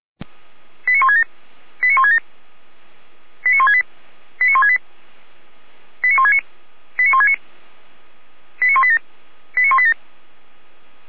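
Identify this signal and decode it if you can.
Five-tone selective call (ZVEI), heard as quick runs of five tones, tones of about 80 ms. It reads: 89167 89167 89168 89168 89169 89169 89187 89187
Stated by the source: ZVEI five-tone selective call received on SDR.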